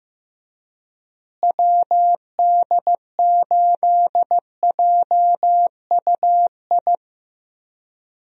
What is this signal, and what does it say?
Morse code, heard as a keyed tone, about 14 wpm, 700 Hz. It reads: WD8JUI
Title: USA Call Signs - 6 Characters - Repeated 3x - 15wpm